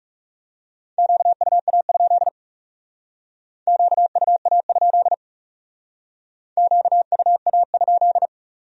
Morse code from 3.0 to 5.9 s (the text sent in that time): QUA?